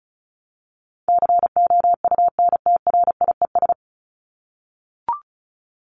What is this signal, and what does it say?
Morse code, heard as a keyed tone, about 34 wpm, 700 Hz. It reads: COUNTRIES
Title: ES:S Suffix - Top 100 Words with Base Words 35wpm